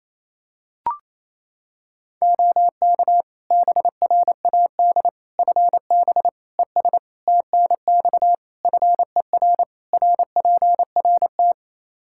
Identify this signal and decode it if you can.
Morse code, heard as a keyed tone, about 28 wpm, 700 Hz. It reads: OK BRAD FB ES TNX FER RPRT
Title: K1USN SST QSO Practice 28wpm